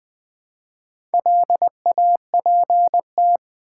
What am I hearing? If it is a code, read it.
Morse code, heard as a keyed tone, about 20 wpm, 700 Hz. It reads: LAPT
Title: Build A Word - 2 Letters At A Time - 6-Letter Words - 20wpm